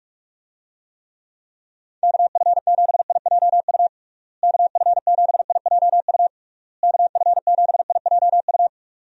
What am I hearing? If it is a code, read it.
Morse code, heard as a keyed tone, about 46 wpm, 700 Hz. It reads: KF7IJU KF7IJU KF7IJU
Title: USA Call Signs - 6 Characters - Repeated 3x - 45wpm